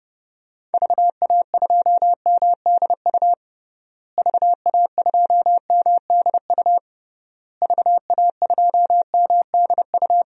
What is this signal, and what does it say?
Morse code, heard as a keyed tone, about 30 wpm, 700 Hz. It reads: VA2MDU VA2MDU VA2MDU
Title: DXCC-100 Call Signs - 6 Characters - Repeated 3x - 30wpm